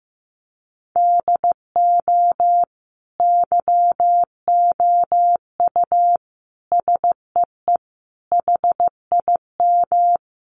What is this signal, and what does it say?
Morse code, heard as a keyed tone, about 15 wpm, 700 Hz. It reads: DO YOU SEE HIM
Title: Sentences from Top 1400 Words 15wpm